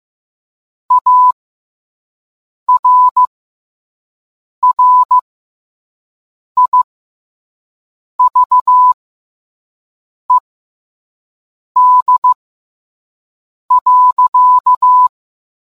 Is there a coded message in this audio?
Yes: Morse code from 0.9 to 15.1 s, heard as a keyed tone, about 15 wpm, spaced as for 5 wpm, 1000 Hz. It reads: ARRIVED.